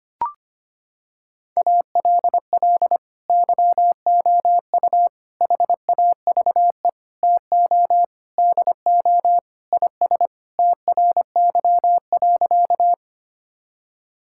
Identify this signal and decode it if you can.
Morse code, heard as a keyed tone, about 25 wpm, 700 Hz. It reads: ALL YOU HAVE TO DO IS TRY.
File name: practice / 25wpm / Sentences with 3-Word Phrases - 25wpm